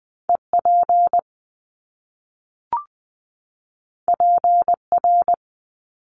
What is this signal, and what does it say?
Morse code, heard as a keyed tone, about 20 wpm, 700 Hz. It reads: EP PR
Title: Top Two Letter Combinations in Words 20wpm